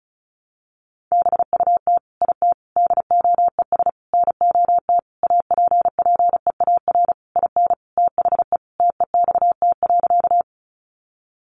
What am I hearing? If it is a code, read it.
Morse code, heard as a keyed tone, about 34 wpm, 700 Hz. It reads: BUT IT DOES NOT APPEAR IN THE TEXT.
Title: Sentences from Top 1200 Words 35wpm